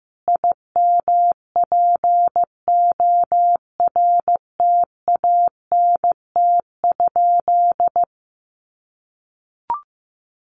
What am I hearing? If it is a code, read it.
Morse code, heard as a keyed tone, about 15 wpm, 700 Hz. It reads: IMPORTANT?
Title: Sentences from Top 1300 Words 15wpm